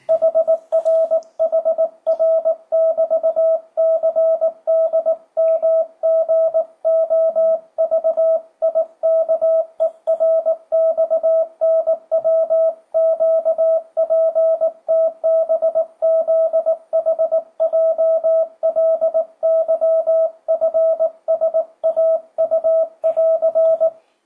Morse code, heard as a keyed tone, about 18 wpm, 660 Hz. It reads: HRHR<BT>CDMGOVIKERXNWQPTBZHJLYFSAU<AR>
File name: 2025cw / speedup / 19wpm